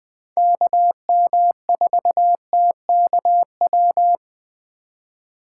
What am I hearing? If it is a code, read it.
Morse code, heard as a keyed tone, about 20 wpm, 700 Hz. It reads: KM4TKW